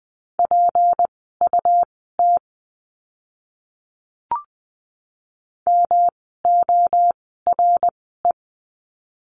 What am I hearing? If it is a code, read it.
Morse code, heard as a keyed tone, about 20 wpm, spaced as for 14 wpm, 700 Hz. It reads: PUT MORE
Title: 237 Lesson 38f - Letter q - Words 20wpm